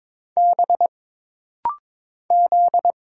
Morse code, heard as a keyed tone, about 22 wpm, 700 Hz. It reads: B Z
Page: This rapid-fire timing is geared to push your ICR (Instant Character Recognition) skills.